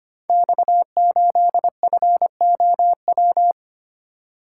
Morse code, heard as a keyed tone, about 25 wpm, 700 Hz. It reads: X8FOW
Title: Code Groups - 5 Letter:Numbers - 25wpm